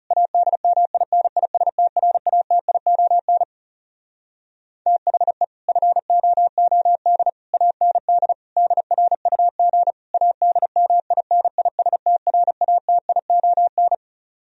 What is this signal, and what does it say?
Morse code, heard as a keyed tone, about 40 wpm, 700 Hz. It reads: ADMINISTRATION THE FOOD AND DRUG ADMINISTRATION